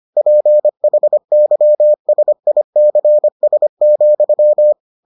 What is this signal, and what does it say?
Morse code, heard as a keyed tone, about 25 wpm, 600 Hz. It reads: PHYSICS,